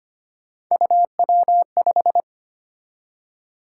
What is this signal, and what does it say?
Morse code, heard as a keyed tone, about 29 wpm, 700 Hz. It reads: UW5
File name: Speed-Racing - Three Letter-Number Combinations - 25wpm